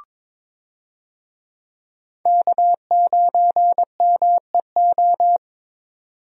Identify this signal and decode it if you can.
Morse code, heard as a keyed tone, about 22 wpm, 700 Hz. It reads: K9MEO
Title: USA Call Signs - 5 Characters - 22wpm